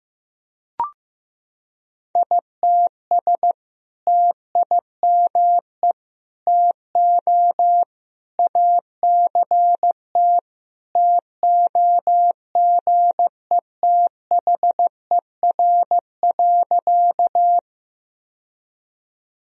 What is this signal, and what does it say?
Morse code, heard as a keyed tone, about 15 wpm, 700 Hz. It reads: ITS TIME TO ACT TOGETHER.
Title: Sentences from Top 1000 Words - Review Part 1 - 15wpm